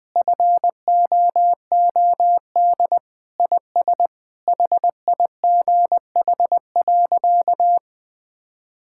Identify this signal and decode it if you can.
Morse code, heard as a keyed tone, about 19 wpm, 700 Hz. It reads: FOOD IS HIGH.
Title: Sentences with 3-Word Phrases - Spoken Beginning - 20wpm